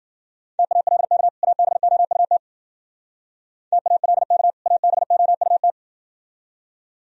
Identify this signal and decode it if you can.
Morse code, heard as a keyed tone, about 60 wpm, 700 Hz. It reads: TALK ABOUT TALK ABOUT